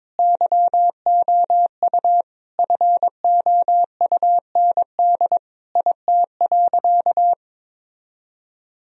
Morse code, heard as a keyed tone, about 22 wpm, 700 Hz. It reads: YOU FOUND IT.
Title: Sentences from Top 1000 Words - Review Part 4 - 22wpm